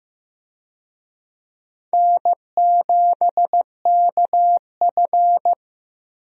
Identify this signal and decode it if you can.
Morse code, heard as a keyed tone, about 16 wpm, 700 Hz. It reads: N7KF
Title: USA Call Signs - 4 Characters - Repeated 3x - 15wpm